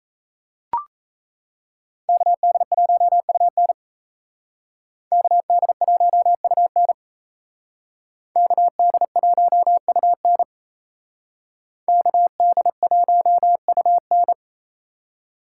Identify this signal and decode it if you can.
Morse code, heard as a keyed tone, about 35 wpm, 700 Hz. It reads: KD1UN KD1UN KD1UN KD1UN